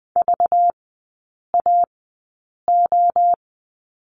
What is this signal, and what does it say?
Morse code, heard as a keyed tone, about 20 wpm, spaced as for 8 wpm, 690 Hz. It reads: VAO
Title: 191 Lesson 31e - Letter v - Three Characters 20wpm